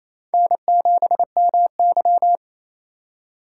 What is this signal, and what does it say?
Morse code, heard as a keyed tone, about 28 wpm, 700 Hz. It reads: N7MY
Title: USA Call Signs - 4 Characters - 6x Character Spacing - 28wpm